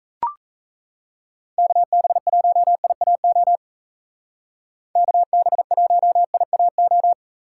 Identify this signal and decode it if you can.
Morse code, heard as a keyed tone, about 35 wpm, 700 Hz. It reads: KB1IAO KB1IAO
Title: Speed Racing - USA 2x3 Call Signs 28wpm